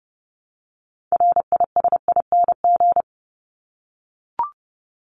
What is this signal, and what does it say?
Morse code, heard as a keyed tone, about 30 wpm, 700 Hz. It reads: RISING
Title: ING Suffix - Top 100 Words with Base Words 30wpm